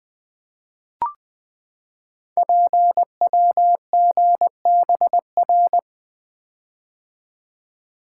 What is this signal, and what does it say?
Morse code, heard as a keyed tone, about 20 wpm, 700 Hz. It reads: PWGBR
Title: Code Groups - 5 Letters - 20wpm